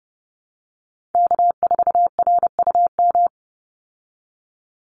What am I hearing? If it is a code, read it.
Morse code, heard as a keyed tone, about 28 wpm, 700 Hz. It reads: K4RUM
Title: Call Signs Repeated 3x - 30wpm